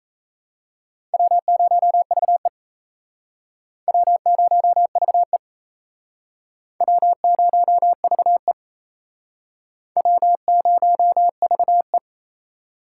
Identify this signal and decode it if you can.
Morse code, heard as a keyed tone, about 34 wpm, 700 Hz. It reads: W0VE W0VE W0VE W0VE
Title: Speed Racing - USA 1x2 Call Signs 28wpm